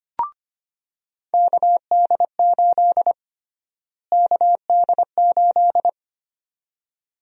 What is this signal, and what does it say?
Morse code, heard as a keyed tone, about 25 wpm, 700 Hz. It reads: KD8 KD8
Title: USA Call Sign Prefixes - Repeated 2x - 25wpm